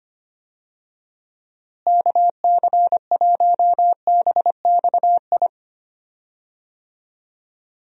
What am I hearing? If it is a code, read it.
Morse code, heard as a keyed tone, about 25 wpm, 700 Hz. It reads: KC1BXI
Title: USA Call Signs - 6 Characters - Repeated 2x - 25wpm